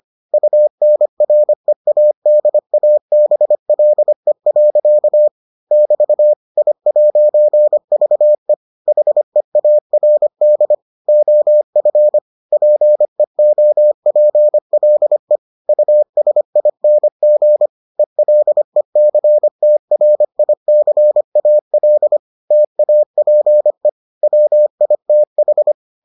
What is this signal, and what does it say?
Morse code, heard as a keyed tone, about 25 wpm, 600 Hz. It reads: UNREADABLE. <BT> I'VE HEARD OF PEOPLE USING ELECTRICAL TAPE WITH